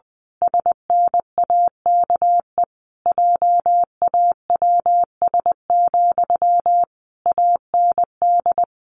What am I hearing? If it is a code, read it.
Morse code, heard as a keyed tone, about 20 wpm, 700 Hz. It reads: SNAKE JAWS, AND
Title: A Sound Of Thunder 20wpm